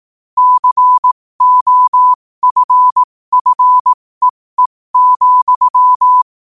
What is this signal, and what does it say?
Morse code, heard as a keyed tone, about 18 wpm, spaced as for 15 wpm, 1000 Hz. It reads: COFFEE,
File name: qotdfiles / 15wpm